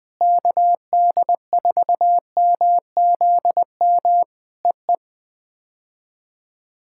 Morse code, heard as a keyed tone, about 19 wpm, 700 Hz. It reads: KD4MZM EE
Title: POTA QSO Practice - 20wpm